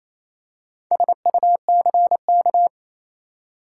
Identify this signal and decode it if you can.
Morse code, heard as a keyed tone, about 28 wpm, 700 Hz. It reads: SUCK
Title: Four Letter Words - 28wpm